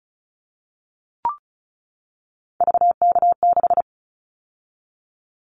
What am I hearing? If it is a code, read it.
Morse code, heard as a keyed tone, about 36 wpm, 700 Hz. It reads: VK6
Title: Callsign Prefixes for Top 100 DXCC - 35wpm